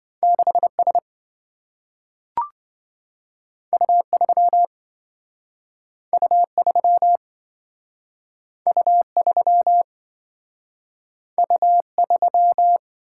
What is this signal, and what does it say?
Morse code, heard as a keyed tone, about 27 wpm, 700 Hz. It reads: 6S U3 U3 U3 U3